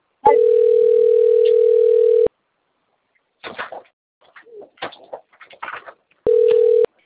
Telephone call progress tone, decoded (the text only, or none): ringback tone